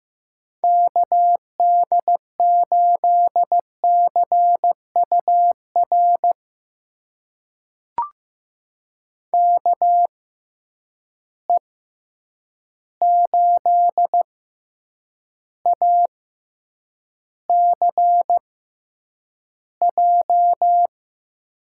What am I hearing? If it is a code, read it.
Morse code, heard as a keyed tone, about 15 wpm, spaced as for 8 wpm, 700 Hz. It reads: KD8CUR KE8ACJ
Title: USA Call Signs - 6 Characters - 6x Character Spacing - 15wpm